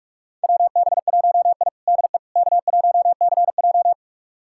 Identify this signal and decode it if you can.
Morse code, heard as a keyed tone, about 47 wpm, 700 Hz. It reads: WB1I DE K1CJ
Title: USA Callsign DE Callsign - 4 Characters - 45wpm